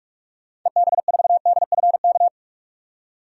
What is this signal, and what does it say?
Morse code, heard as a keyed tone, about 48 wpm, 700 Hz. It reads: EB4DFK
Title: DXCC-100 Call Signs - 6 Characters - 45wpm